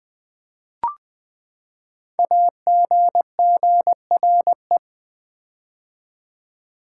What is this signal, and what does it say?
Morse code, heard as a keyed tone, about 20 wpm, 700 Hz. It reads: AGGRE
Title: Build A Word - 5 Letters At A Time - 10-Letter Words - 20wpm